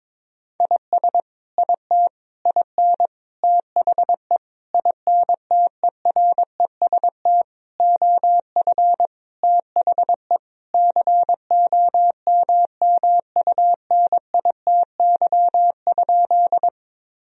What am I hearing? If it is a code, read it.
Morse code, heard as a keyed tone, about 22 wpm, 700 Hz. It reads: IS IT IN THE INTEREST OF THE COMMUNITY?
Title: Sentences from Top 1000 Words - Review Part 4 - 22wpm